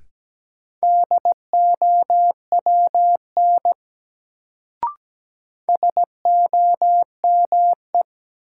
Morse code, heard as a keyed tone, about 17 wpm, 700 Hz. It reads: DOWN SOME
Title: Sets of 2 Words - New Word Drill - 17wpm